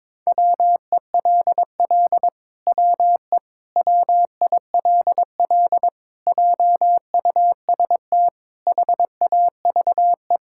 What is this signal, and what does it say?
Morse code, heard as a keyed tone, about 21 wpm, 700 Hz. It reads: WELL WE WILL JUST HAVE
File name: Sentences with 7-Word Phrases - 22wpm